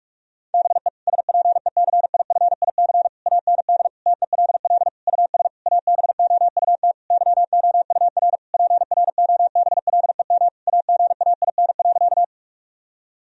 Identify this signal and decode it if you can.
Morse code, heard as a keyed tone, about 46 wpm, 700 Hz. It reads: BE SPECIFIC AND TELL US ABOUT YOUR PROBLEM AGAIN.
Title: Sentences from Top 1000 Words - Review Part 3 - 45wpm